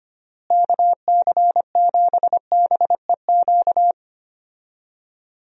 Morse code, heard as a keyed tone, about 25 wpm, 700 Hz. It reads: KC7BEQ
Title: USA Call Signs - 6 Characters - Repeated 2x - 25wpm